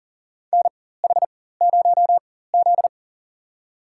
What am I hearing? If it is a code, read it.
Morse code, heard as a keyed tone, about 40 wpm, spaced as for 19 wpm, 700 Hz. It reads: NH0Z
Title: USA Call Signs - 4x Character Spacing - 40wpm